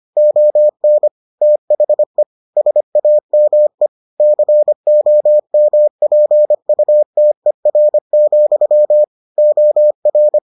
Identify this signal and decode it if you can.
Morse code, heard as a keyed tone, about 25 wpm, 600 Hz. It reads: ON THE SAME COMPUTER, OR